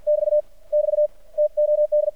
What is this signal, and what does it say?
Morse code, heard as a keyed tone, about 54 wpm, 600 Hz. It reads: <BT> <BT> TOD
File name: archived / CRSnet